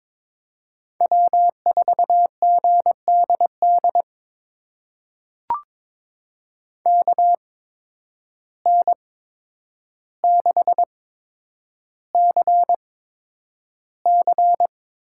Morse code, heard as a keyed tone, about 22 wpm, spaced as for 9 wpm, 700 Hz. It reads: W4GDD KN6CC
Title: USA Call Signs - 5 Characters - 8x Character Spacing - 22wpm